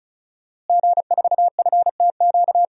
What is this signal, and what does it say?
Morse code, heard as a keyed tone, about 36 wpm, 700 Hz. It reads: G4FTQ
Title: DXCC-100 Call Signs - 5 Characters - Repeated 2x - 35wpm